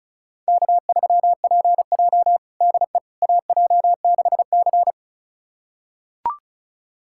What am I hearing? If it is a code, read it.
Morse code, heard as a keyed tone, about 34 wpm, 700 Hz. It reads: K3PJ DE AJ6C
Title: USA Callsign DE Callsign - 4 Characters - 35wpm